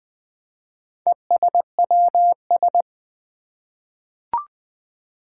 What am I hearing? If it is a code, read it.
Morse code, heard as a keyed tone, about 20 wpm, 700 Hz. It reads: ESWS
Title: Four Challenging Letter-Number Combinations 20wpm